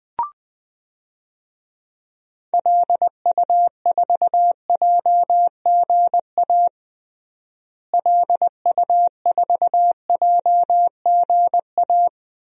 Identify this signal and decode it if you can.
Morse code, heard as a keyed tone, about 20 wpm, 700 Hz. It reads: LU4JGA LU4JGA
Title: DXCC-100 Call Signs - 6 Characters - Repeated 3x - 20wpm